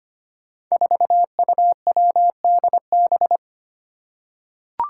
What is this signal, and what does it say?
Morse code, heard as a keyed tone, about 25 wpm, 700 Hz. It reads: 4UWDB
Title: Five Challenging Letter-Number Combinations 25wpm